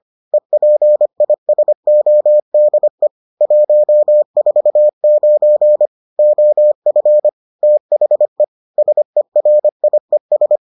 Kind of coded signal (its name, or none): Morse code